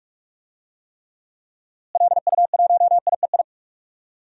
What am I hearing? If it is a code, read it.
Morse code, heard as a keyed tone, about 41 wpm, 700 Hz. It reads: LU1IEI
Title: DXCC-100 Call Signs - 6 Characters - Repeated 2x - 45wpm